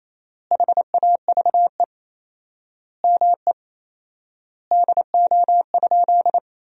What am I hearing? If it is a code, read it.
Morse code, heard as a keyed tone, about 28 wpm, 700 Hz. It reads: HAVE ME DO?
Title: 4x Word-Spacing - Sentences from Top 200 Words - 28wpm